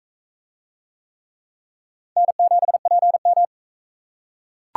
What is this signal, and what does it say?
Morse code, heard as a keyed tone, about 36 wpm, 700 Hz. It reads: N7PM